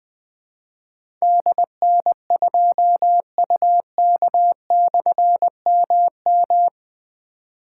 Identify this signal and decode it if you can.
Morse code, heard as a keyed tone, about 20 wpm, 700 Hz. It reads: DN2UK/MM
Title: Difficult Call Signs 20wpm